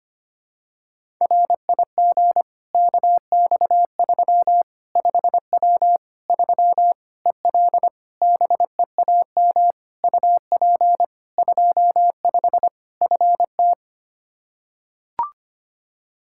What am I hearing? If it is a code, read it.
Morse code, heard as a keyed tone, about 25 wpm, 700 Hz. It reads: RIG KX3 5W 3 EL BEAM UP 25 FT
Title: K1USN SST QSO Practice 25wpm